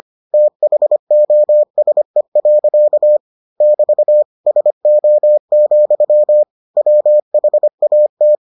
Morse code, heard as a keyed tone, about 25 wpm, 600 Hz. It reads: THOSE. <BT> SO, WHAT